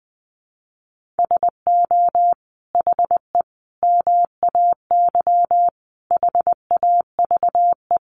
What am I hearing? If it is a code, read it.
Morse code, heard as a keyed tone, about 20 wpm, 700 Hz. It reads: SO HE MAY HAVE
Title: Sentences from Top 1500 Words 20wpm